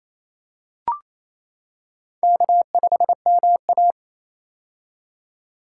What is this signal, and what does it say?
Morse code, heard as a keyed tone, about 27 wpm, 700 Hz. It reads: K5MA
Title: K1USN SST QSO Practice - 28wpm